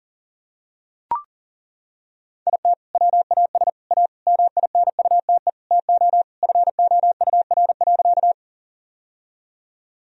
Morse code, heard as a keyed tone, about 40 wpm, 700 Hz. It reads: IT WAS A MINUTE TO FOUR.
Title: Sentences from Top 1000 Words - Review Part 1 - 40wpm